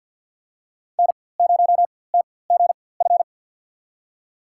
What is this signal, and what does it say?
Morse code, heard as a keyed tone, about 50 wpm, spaced as for 23 wpm, 700 Hz. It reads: N0TGF
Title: USA Call Signs - 5 Characters - 4x Character Spacing - 50wpm